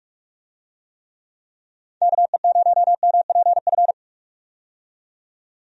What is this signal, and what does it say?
Morse code, heard as a keyed tone, about 43 wpm, 700 Hz. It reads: KE0MPF
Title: USA Call Signs - 6 Characters - 45wpm